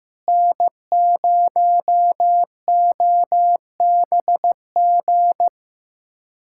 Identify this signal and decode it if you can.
Morse code, heard as a keyed tone, about 15 wpm, 700 Hz. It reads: N0OBG
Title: USA Callsign DE Callsign 15wpm